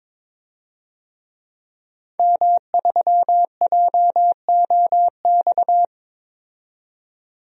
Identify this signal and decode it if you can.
Morse code, heard as a keyed tone, about 22 wpm, 700 Hz. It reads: M3JOX
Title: DXCC-100 Call Signs - 5 Characters - 22wpm